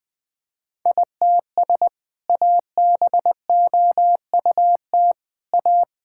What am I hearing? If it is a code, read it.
Morse code, heard as a keyed tone, about 20 wpm, 700 Hz. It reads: ITS ABOUT A
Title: Sentences with 2-Word Phrases - 20wpm